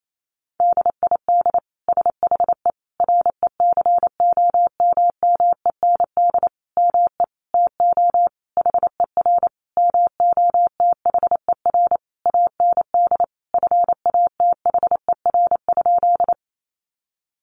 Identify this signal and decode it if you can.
Morse code, heard as a keyed tone, about 28 wpm, 700 Hz. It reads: DID SHE RECOMMEND ME TO HER MOTHER AND FATHER?
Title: Sentences from Top 1200 Words 28wpm